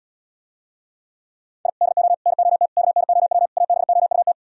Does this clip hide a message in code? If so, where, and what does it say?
Yes: Morse code from 1.7 to 4.3 s, heard as a keyed tone, about 78 wpm, 700 Hz. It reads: I DO NOT KNOW ABOUT